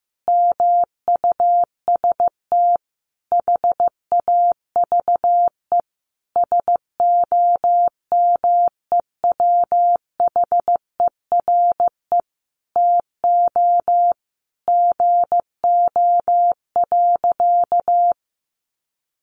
Morse code, heard as a keyed tone, about 15 wpm, 700 Hz. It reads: MUST HAVE SOMEWHERE TO GO.
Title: Sentences from Top 1400 Words 15wpm